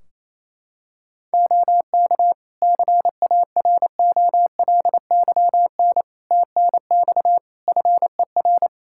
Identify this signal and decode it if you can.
Morse code, heard as a keyed tone, about 28 wpm, 700 Hz. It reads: OK CAROLYN TNX FER
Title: Practice QSO - Exchange 4 - Ending - 28wpm